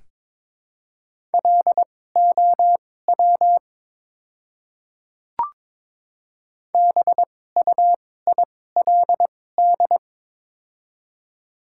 Morse code, heard as a keyed tone, about 22 wpm, spaced as for 16 wpm, 700 Hz. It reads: LOW BUILD
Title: 2x Character Spacing - Top 300 Words - 22wpm